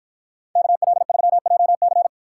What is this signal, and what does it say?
Morse code, heard as a keyed tone, about 53 wpm, 700 Hz. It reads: KL3JC